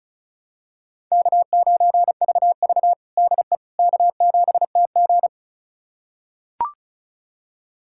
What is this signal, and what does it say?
Morse code, heard as a keyed tone, about 36 wpm, 700 Hz. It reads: K9VV DE K7TG
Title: USA Callsign DE Callsign - 4 Characters - 35wpm